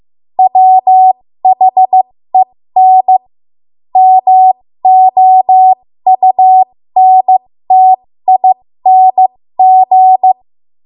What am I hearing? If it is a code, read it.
Morse code, heard as a keyed tone, about 15 wpm, spaced as for 13 wpm, 750 Hz. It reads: WHEN MOUNTING
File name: Morse / 13WPM